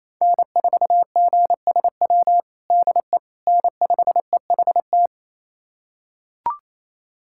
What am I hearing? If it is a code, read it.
Morse code, heard as a keyed tone, about 28 wpm, 700 Hz. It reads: N4GSW DE N5EHT
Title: USA Callsign DE Callsign - 5 Characters - 28wpm